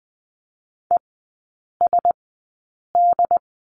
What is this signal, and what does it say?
Morse code, heard as a keyed tone, about 20 wpm, spaced as for 8 wpm, 700 Hz. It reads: ESD
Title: 067 Lesson 12e - Letter d - Three Characters 20wpm